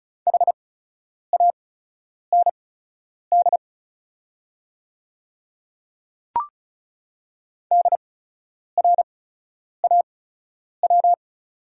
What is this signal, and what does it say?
Morse code, heard as a keyed tone, about 34 wpm, spaced as for 10 wpm, 700 Hz. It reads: HAND DRAW
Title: 8x Character Spacing - Top 300 Words - 35wpm